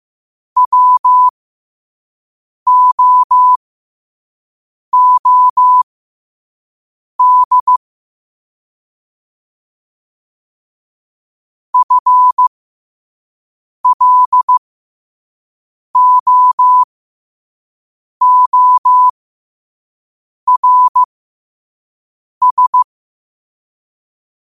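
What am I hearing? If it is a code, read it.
Morse code, heard as a keyed tone, about 15 wpm, spaced as for 5 wpm, 1000 Hz. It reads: WOOD FLOORS
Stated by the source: Morse code practice at 5 wpm.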